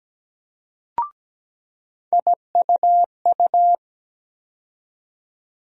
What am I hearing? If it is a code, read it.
Morse code, heard as a keyed tone, about 17 wpm, 700 Hz. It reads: IUU
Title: USA Call Sign Postfixes - 3 Letters 17wpm